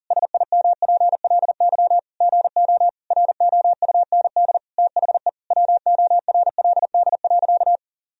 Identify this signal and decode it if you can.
Morse code, heard as a keyed tone, about 40 wpm, 700 Hz. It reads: SIMPLY GO ROUND THE WORLD.
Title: Sentences from Top 1000 Words - Review Part 4 - 40wpm